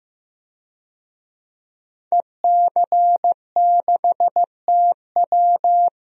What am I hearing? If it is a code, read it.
Morse code, heard as a keyed tone, about 16 wpm, 700 Hz. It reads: EC6TW
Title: DXCC-100 Call Signs - 5 Characters - Repeated 2x - 15wpm